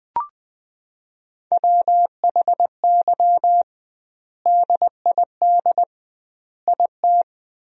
Morse code, heard as a keyed tone, about 20 wpm, 700 Hz. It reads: WHY DID IT
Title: 2x Word-Spacing - Sentences from Top 400 Words - 20wpm